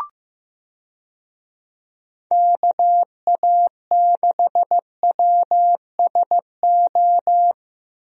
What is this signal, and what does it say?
Morse code, heard as a keyed tone, about 15 wpm, 700 Hz. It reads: KA6WSO
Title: USA Call Signs - 6 Characters - Repeated 2x - 15wpm